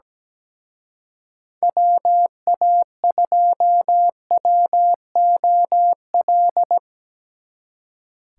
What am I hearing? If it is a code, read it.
Morse code, heard as a keyed tone, about 19 wpm, 700 Hz. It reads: WA2WOL
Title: Speed Racing - USA 2x3 Call Signs 17wpm